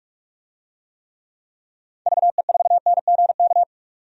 Morse code, heard as a keyed tone, about 46 wpm, 700 Hz. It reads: VE4NGK
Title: DXCC-100 Call Signs - 6 Characters - 45wpm